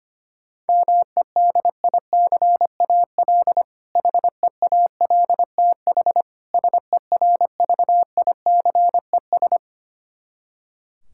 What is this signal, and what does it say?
Morse code, heard as a keyed tone, about 25 wpm, 700 Hz. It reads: MEDICAL HEALTH SERVICES
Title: K1USN SST QSO Practice 25wpm